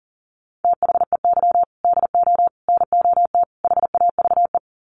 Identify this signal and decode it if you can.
Morse code, heard as a keyed tone, about 40 wpm, 700 Hz. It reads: THEY DO NOT HAVE
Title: Sets of 4 Words 40wpm